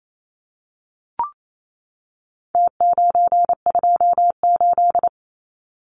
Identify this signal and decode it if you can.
Morse code, heard as a keyed tone, about 28 wpm, 700 Hz. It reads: T928
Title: Numbers - Contest Style - 4-Digit 28wpm